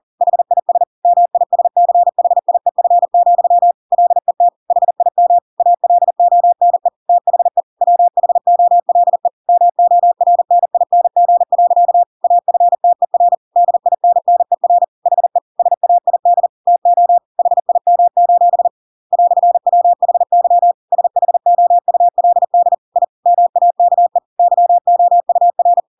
Morse code, heard as a keyed tone, about 40 wpm, 700 Hz. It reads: HIS MISCHIEF, LET HIM ALONE THE WHOLE MORNING. AFTER DINNER HE SAID TO HIM: "WHY SHOULD I MAKE YOUR